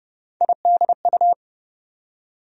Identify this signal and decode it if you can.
Morse code, heard as a keyed tone, about 30 wpm, 700 Hz. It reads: IDU
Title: USA Call Sign Postfixes - 3 Letters 30wpm